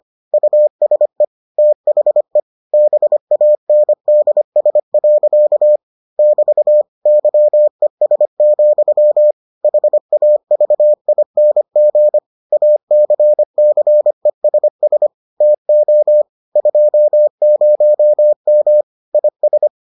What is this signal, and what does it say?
Morse code, heard as a keyed tone, about 25 wpm, 600 Hz. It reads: USE THE BANDS. <BT> YES, HAVING ACCESS TO 20M IS